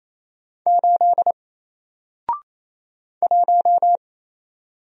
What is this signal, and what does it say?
Morse code, heard as a keyed tone, about 27 wpm, 700 Hz. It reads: 8 1
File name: Single-Letter-Number-28wpm